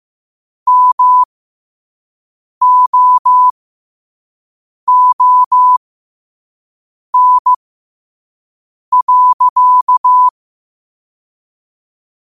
Morse code, heard as a keyed tone, about 15 wpm, spaced as for 5 wpm, 1000 Hz. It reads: MOON.
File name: qotdfiles / 5wpm